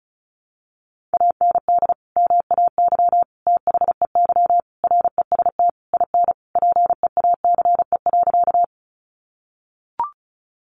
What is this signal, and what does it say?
Morse code, heard as a keyed tone, about 34 wpm, 700 Hz. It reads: AND MAY THEY REST IN PEACE.
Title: Sentences from Top 1500 Words 35wpm